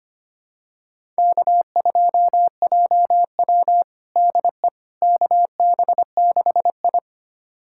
Morse code, heard as a keyed tone, about 25 wpm, 700 Hz. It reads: K2JW DE KB6I
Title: USA Callsign DE Callsign - 4 Characters - 25wpm